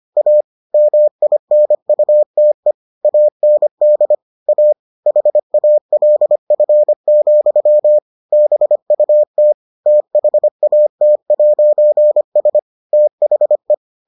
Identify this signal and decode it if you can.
Morse code, heard as a keyed tone, about 25 wpm, 600 Hz. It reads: A MINUTE AND A HALF, BUT THAT'S THE